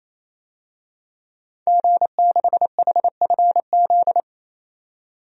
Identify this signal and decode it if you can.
Morse code, heard as a keyed tone, about 28 wpm, 700 Hz. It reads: G6HFZ
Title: DXCC-100 Call Signs - 5 Characters - 28wpm